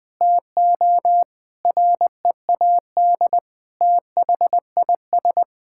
Morse code, heard as a keyed tone, about 20 wpm, 700 Hz. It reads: TO READ THIS
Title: Sentences with 9-Word Phrases - Spoken Beginning - 20wpm